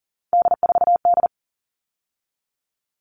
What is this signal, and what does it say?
Morse code, heard as a keyed tone, about 40 wpm, 700 Hz. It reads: D4D
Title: All Cut Numbers - 3-Digit 40wpm